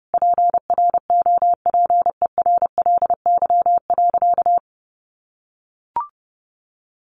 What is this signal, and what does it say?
Morse code, heard as a keyed tone, about 28 wpm, 700 Hz. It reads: PROPERLY.
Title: Sentences from Top 1100 Words 30wpm